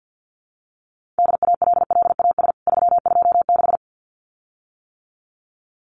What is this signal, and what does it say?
Morse code, heard as a keyed tone, about 50 wpm, 700 Hz. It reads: DALLAS 326
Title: CWT Exchange 50wpm